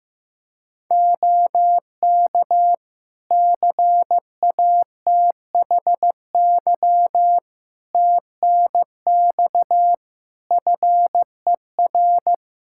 Morse code, heard as a keyed tone, about 15 wpm, 700 Hz. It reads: OK CATHY TNX FER
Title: Practice QSO - Exchange 4 - Ending - 15wpm